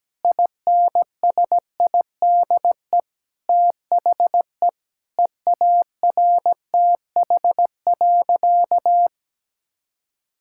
Morse code, heard as a keyed tone, about 17 wpm, 700 Hz. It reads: INSIDE THE EARTH.
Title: Sentences from Top 1000 Words - Review Part 6 - 17wpm